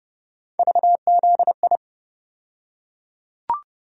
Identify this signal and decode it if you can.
Morse code, heard as a keyed tone, about 31 wpm, 700 Hz. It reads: VZI
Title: USA Call Sign Postfixes - 3 Letters 30wpm